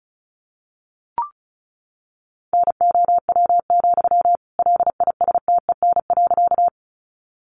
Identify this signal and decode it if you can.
Morse code, heard as a keyed tone, about 34 wpm, 700 Hz. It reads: NOW, LISTEN.
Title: Sentences from Top 400 Words 35wpm